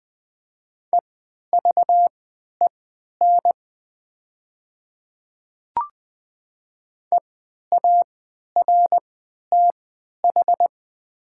Morse code, heard as a keyed tone, about 20 wpm, spaced as for 11 wpm, 700 Hz. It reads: EVEN EARTH
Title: 3x Character Spacing - Top 300 Words - 20wpm